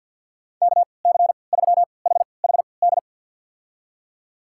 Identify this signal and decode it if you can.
Morse code, heard as a keyed tone, about 52 wpm, spaced as for 29 wpm, 700 Hz. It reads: KC3HHD
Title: USA Call Signs - 6 Characters - 3x Character Spacing - 50wpm